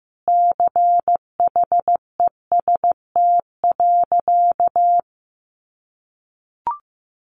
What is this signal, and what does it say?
Morse code, heard as a keyed tone, about 15 wpm, 700 Hz. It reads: CHEST.
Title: A Sound Of Thunder 15wpm